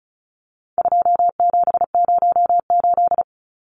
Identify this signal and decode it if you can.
Morse code, heard as a keyed tone, about 34 wpm, 700 Hz. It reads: 2708
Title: Numbers - 4-Digit 35wpm